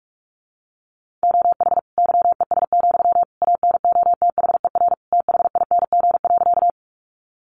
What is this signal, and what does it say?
Morse code, heard as a keyed tone, about 44 wpm, 700 Hz. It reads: OH YES, ANOTHER THING.
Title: Sentences from Top 600 Words 45wpm